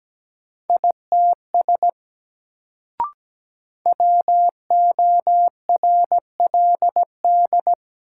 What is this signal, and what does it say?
Morse code, heard as a keyed tone, about 17 wpm, 700 Hz. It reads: ITS WORLD
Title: Sets of 2 Words - New Word Drill - 17wpm